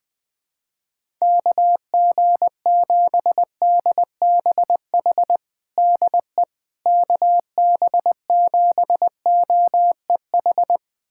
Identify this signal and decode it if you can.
Morse code, heard as a keyed tone, about 20 wpm, 700 Hz. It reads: KG7DBH DE KB7OEH
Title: USA Callsign DE Callsign - 6 Characters - 20wpm